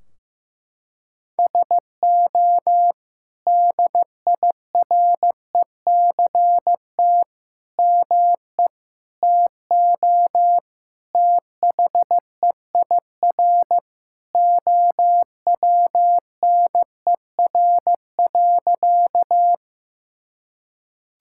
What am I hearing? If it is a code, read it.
Morse code, heard as a keyed tone, about 15 wpm, 700 Hz. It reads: SO DIRECT ME TO THEIR OWNER.
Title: Sentences from Top 1600 Words 15wpm